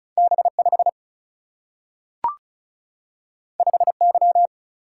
Morse code, heard as a keyed tone, about 36 wpm, 700 Hz. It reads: B5 5Y
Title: Single-Letter Single-Number 35wpm